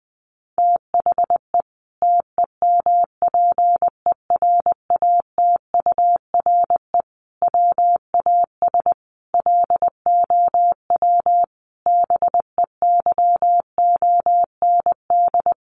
Morse code, heard as a keyed tone, about 20 wpm, 700 Hz. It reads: THE TEMPERATURE WAS LOW BEYOND
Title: Sentences from Top 1500 Words 20wpm